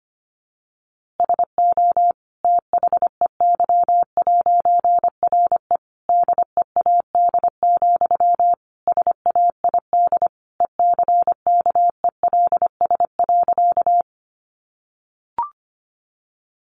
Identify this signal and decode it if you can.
Morse code, heard as a keyed tone, about 25 wpm, 700 Hz. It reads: SO THEY'RE DEAD, SAID ECKELS.
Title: A Sound Of Thunder 25wpm